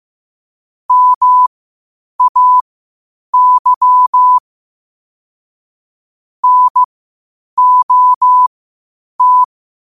Morse code, heard as a keyed tone, about 15 wpm, spaced as for 8 wpm, 1000 Hz. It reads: MAY NOT
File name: qotdfiles / 8wpm